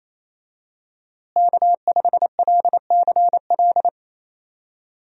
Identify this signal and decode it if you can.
Morse code, heard as a keyed tone, about 27 wpm, 700 Hz. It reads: K5LCL
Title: USA Call Signs - 5 Characters - Repeated 3x - 28wpm